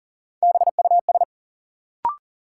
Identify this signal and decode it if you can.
Morse code, heard as a keyed tone, about 40 wpm, 700 Hz. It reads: BUS
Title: Sentences from Top 1000 Words - Review Part 6 - New Word Drill - 40wpm